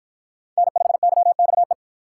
Morse code, heard as a keyed tone, about 53 wpm, 700 Hz. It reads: N5YXE